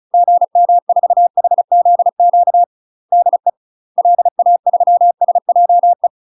Morse code, heard as a keyed tone, about 34 wpm, 700 Hz. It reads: GM4HZQ DE LA3SJE
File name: UK MASTER QSOs TRAINING / 35WPM / UK MASTER QSOs 0001